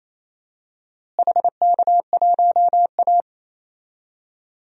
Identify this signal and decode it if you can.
Morse code, heard as a keyed tone, about 28 wpm, 700 Hz. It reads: HK1A
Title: DXCC-100 Call Signs - 4 Characters - Repeated 2x - 28wpm